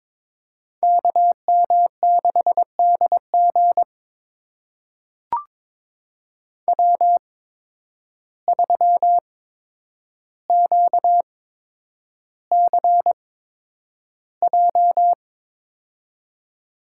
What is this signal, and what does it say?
Morse code, heard as a keyed tone, about 22 wpm, spaced as for 9 wpm, 700 Hz. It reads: KM6DG W3QCJ
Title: USA Call Signs - 8x Character Spacing - 22wpm